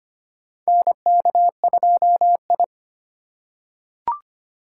Morse code, heard as a keyed tone, about 25 wpm, 700 Hz. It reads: NK2I
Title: USA Call Signs - 4 Characters - 8x Character Spacing - 25wpm